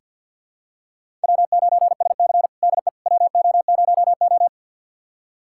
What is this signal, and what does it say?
Morse code, heard as a keyed tone, about 50 wpm, 700 Hz. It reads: W9SC DE WO0O
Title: USA Callsign DE Callsign - 4 Characters - 50wpm